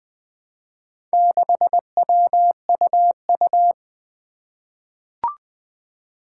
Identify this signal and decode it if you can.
Morse code, heard as a keyed tone, about 20 wpm, 700 Hz. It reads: 6WUU
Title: Four Challenging Letter-Number Combinations 20wpm